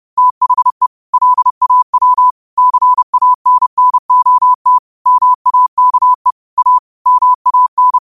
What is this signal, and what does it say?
Morse code, heard as a keyed tone, about 30 wpm, 1000 Hz. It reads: THE LAW CANNOT MAKE A MAN
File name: qotdfiles / 30wpm